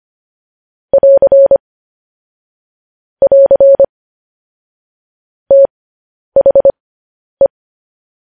Morse code, heard as a keyed tone, about 25 wpm, spaced as for 10 wpm, 550 Hz. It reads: <AR> <AR> THE
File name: podcast / News headlines-1x-25x10